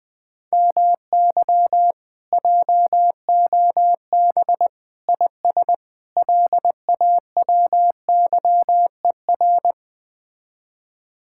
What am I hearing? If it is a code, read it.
Morse code, heard as a keyed tone, about 20 wpm, 700 Hz. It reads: MY JOB IS LAWYER
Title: Speed-Racing Common QSO Phrases Encore - 17wpm